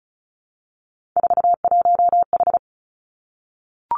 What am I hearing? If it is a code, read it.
Morse code, heard as a keyed tone, about 36 wpm, 700 Hz. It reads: 41H
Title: Three Challenging Letter-Number Combinations 35wpm